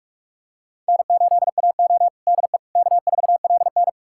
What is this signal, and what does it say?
Morse code, heard as a keyed tone, about 42 wpm, 700 Hz. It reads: N8AO DE K4LN
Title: USA Callsign DE Callsign - 4 Characters - 45wpm